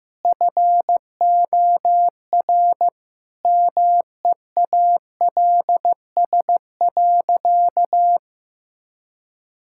Morse code, heard as a keyed tone, about 15 wpm, 700 Hz. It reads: FOR MEALS.
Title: Pangram Challenge v2 - 15wpm